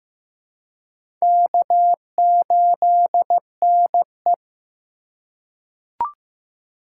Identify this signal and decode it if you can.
Morse code, heard as a keyed tone, about 15 wpm, 700 Hz. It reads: K8NE
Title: USA Call Signs - 4 Characters - 2x Character Spacing - 15wpm